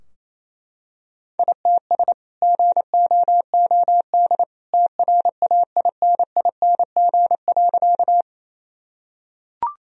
Morse code, heard as a keyed tone, about 28 wpm, 700 Hz. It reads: ITS GOOD TRAINING.
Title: Sentences from Top 1000 Words - Review Part 4 - 28wpm